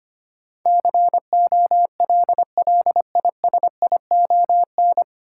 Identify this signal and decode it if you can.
Morse code, heard as a keyed tone, about 25 wpm, 700 Hz. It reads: COLLISION
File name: practice / 25wpm / 9 Letter Words - First 3 Letters Given - 25wpm